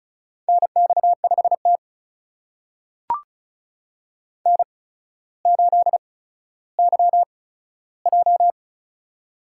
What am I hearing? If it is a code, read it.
Morse code, heard as a keyed tone, about 36 wpm, spaced as for 15 wpm, 700 Hz. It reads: NX5T N8YJ
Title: USA Call Signs - 4 Characters - 8x Character Spacing - 35wpm